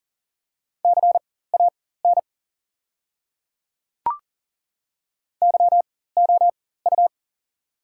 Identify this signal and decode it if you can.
Morse code, heard as a keyed tone, about 40 wpm, spaced as for 19 wpm, 690 Hz. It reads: CAN YOU
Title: 4x Character Spacing - Top 100 Words - 40wpm